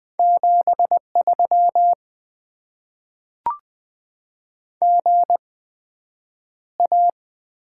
Morse code, heard as a keyed tone, about 20 wpm, spaced as for 9 wpm, 700 Hz. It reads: 73 GA
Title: Top 20 QSO Elements - 8x Character Spacing - 20wpm